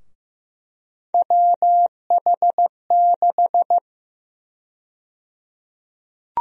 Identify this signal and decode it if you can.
Morse code, heard as a keyed tone, about 15 wpm, 700 Hz. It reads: WH6
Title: USA Call Sign Prefixes - 15wpm